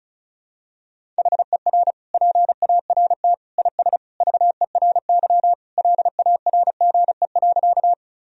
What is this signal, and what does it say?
Morse code, heard as a keyed tone, about 36 wpm, 700 Hz. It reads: HER PART IS VERY LARGE.
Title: Sentences from Top 1000 Words - Review Part 1 - 35wpm